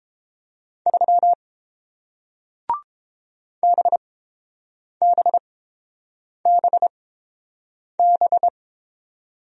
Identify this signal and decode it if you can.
Morse code, heard as a keyed tone, about 30 wpm, spaced as for 7 wpm, 700 Hz. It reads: 3 BBBB